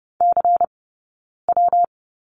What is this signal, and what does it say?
Morse code, heard as a keyed tone, about 30 wpm, 700 Hz. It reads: C W
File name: Lesson 20d - Number 3 - Two Characters 30wpm